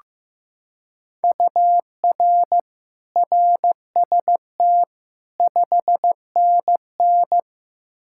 Morse code, heard as a keyed tone, about 15 wpm, 700 Hz. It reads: UR RST 5NN
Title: Practice QSO - Exchange 1 - Beginning - 15wpm